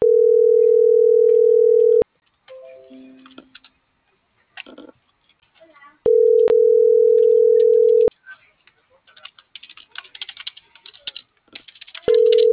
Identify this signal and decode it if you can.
Ringback tone (North America), 440 + 480 Hz.